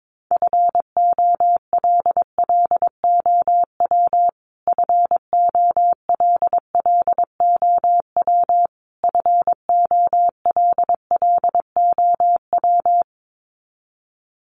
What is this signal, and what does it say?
Morse code, heard as a keyed tone, about 22 wpm, 700 Hz. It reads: FOLLOW FOLLOW FOLLOW
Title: Top 300 Words Repeated 22wpm